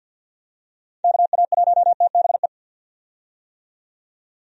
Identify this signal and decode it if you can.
Morse code, heard as a keyed tone, about 51 wpm, 700 Hz. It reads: KA1TBE
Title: USA Call Signs - 6 Characters - 50wpm